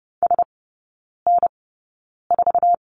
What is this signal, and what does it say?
Morse code, heard as a keyed tone, about 30 wpm, spaced as for 9 wpm, 700 Hz. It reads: SN4
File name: 081 Lesson 14e - Number 2 - Three Characters 30wpm